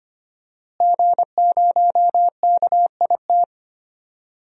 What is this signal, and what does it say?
Morse code, heard as a keyed tone, about 25 wpm, 700 Hz. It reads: G0KIT
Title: DXCC-100 Call Signs - 5 Characters - Repeated 2x - 25wpm